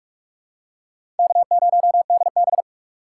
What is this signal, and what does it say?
Morse code, heard as a keyed tone, about 46 wpm, 700 Hz. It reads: K0DB
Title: USA Call Signs - 4 Characters - 4x Character Spacing - 45wpm